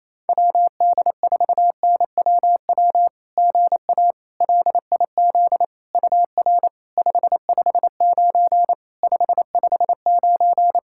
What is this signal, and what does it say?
Morse code, heard as a keyed tone, about 28 wpm, 690 Hz. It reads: WD4NWW GA LIZ UR 559 559
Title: SOTA QSO Practice 28wpm